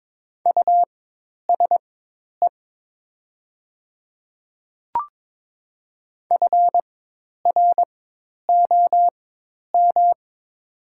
Morse code, heard as a keyed tone, about 22 wpm, spaced as for 10 wpm, 700 Hz. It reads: USE FROM
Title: 4x Character Spacing - Top 200 Words - 22wpm